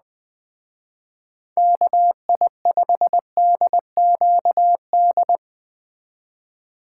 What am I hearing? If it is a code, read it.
Morse code, heard as a keyed tone, about 20 wpm, 700 Hz. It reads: KI5DQD